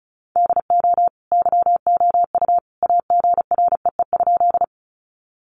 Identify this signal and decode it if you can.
Morse code, heard as a keyed tone, about 34 wpm, 700 Hz. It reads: DO YOU AGREE?
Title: Sentences from Top 1100 Words 35wpm